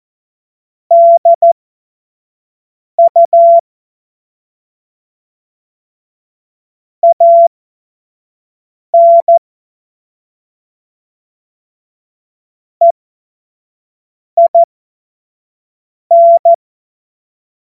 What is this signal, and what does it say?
Morse code, heard as a keyed tone, about 14 wpm, spaced as for 5 wpm, 680 Hz. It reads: DU AN EIN
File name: Morsekurs / Witze / Witz 13. Dezember 2022 25 F70